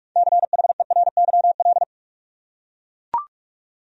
Morse code, heard as a keyed tone, about 41 wpm, 700 Hz. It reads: CHERYL
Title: USA Common Names - 45wpm